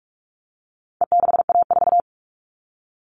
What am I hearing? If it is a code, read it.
Morse code, heard as a keyed tone, about 44 wpm, 700 Hz. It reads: E6A4
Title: All Cut Numbers - 4-Digit 45wpm